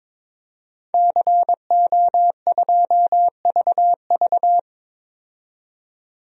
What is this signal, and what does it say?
Morse code, heard as a keyed tone, about 22 wpm, 700 Hz. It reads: CO2VV
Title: DXCC-100 Call Signs - 5 Characters - 22wpm